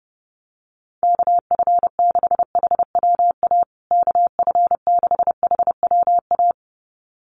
Morse code, heard as a keyed tone, about 30 wpm, 700 Hz. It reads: KF6HWA KF6HWA
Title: USA Call Signs Repeated 2x 30wpm